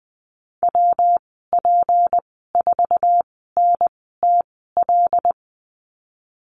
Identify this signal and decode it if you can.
Morse code, heard as a keyed tone, about 20 wpm, spaced as for 14 wpm, 700 Hz. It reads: WP4NTL
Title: 179 Lesson 29g - Slash - Call Signs 20wpm